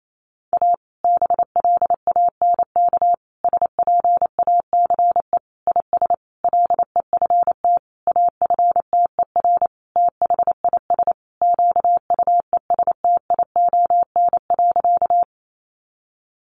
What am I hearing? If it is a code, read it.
Morse code, heard as a keyed tone, about 28 wpm, 700 Hz. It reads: A BLANK SPACE IS LEFT AFTER THIS QUESTION.
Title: Sentences from Top 1500 Words 28wpm